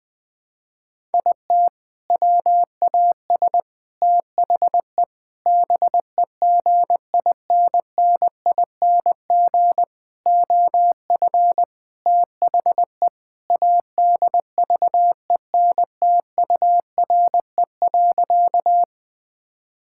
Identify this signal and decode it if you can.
Morse code, heard as a keyed tone, about 20 wpm, 700 Hz. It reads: IT WAS THE BEGINNING OF THE ADVENTURE.
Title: Sentences with 4-Word Phrases - 20wpm